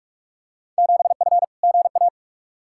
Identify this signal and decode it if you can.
Morse code, heard as a keyed tone, about 45 wpm, 700 Hz. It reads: 7F GA
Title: Field Day Exchanges Encore - 45wpm